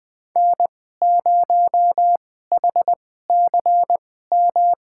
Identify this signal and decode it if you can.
Morse code, heard as a keyed tone, about 21 wpm, spaced as for 15 wpm, 700 Hz. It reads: N0HCM
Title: USA Call Signs - 5 Characters - 2x Character Spacing - 20wpm